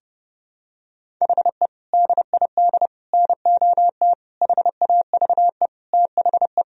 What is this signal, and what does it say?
Morse code, heard as a keyed tone, about 32 wpm, 700 Hz. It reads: HE DID NOT HAVE THE
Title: Sentences from Top 1000 Words - Review Part 1 - 30wpm